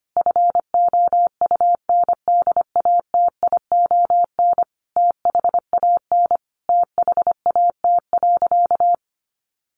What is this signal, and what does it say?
Morse code, heard as a keyed tone, about 25 wpm, 700 Hz. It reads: FOUNDATION THAN THAT.
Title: Sentences from Top 1200 Words 25wpm